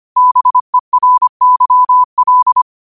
Morse code, heard as a keyed tone, about 25 wpm, 1000 Hz. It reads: BERYL